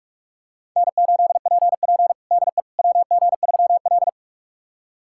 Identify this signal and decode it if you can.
Morse code, heard as a keyed tone, about 46 wpm, 700 Hz. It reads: N8PP DE WG3L
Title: USA Callsign DE Callsign - 4 Characters - 45wpm